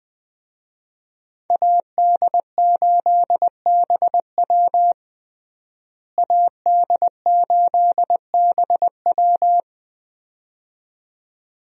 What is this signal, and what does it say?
Morse code, heard as a keyed tone, about 20 wpm, 700 Hz. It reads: AD8BW AD8BW
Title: USA Call Signs - 5 Characters - Repeated 2x - 20wpm